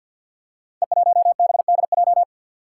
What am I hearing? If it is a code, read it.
Morse code, heard as a keyed tone, about 52 wpm, 690 Hz. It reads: E1BDJ
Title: Code Groups - 5 Letter:Numbers - 50wpm